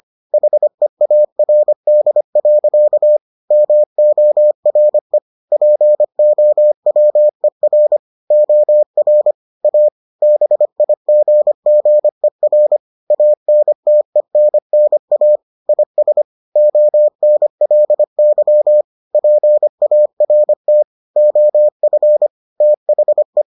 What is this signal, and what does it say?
Morse code, heard as a keyed tone, about 25 wpm, 600 Hz. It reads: HEARD. MORE POWER OR A BIGGER ANTENNA IS ONLY PART OF THE